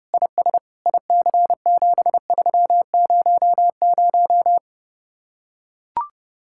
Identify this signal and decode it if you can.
Morse code, heard as a keyed tone, about 32 wpm, 700 Hz. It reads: IS IC7300